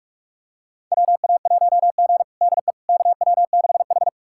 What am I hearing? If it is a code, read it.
Morse code, heard as a keyed tone, about 41 wpm, 700 Hz. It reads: WA1G DE KW6H
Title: USA Callsign DE Callsign - 4 Characters - 45wpm